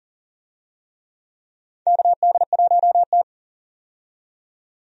Morse code, heard as a keyed tone, about 40 wpm, 700 Hz. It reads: KD1T
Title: USA Call Signs - 4 Characters - 40wpm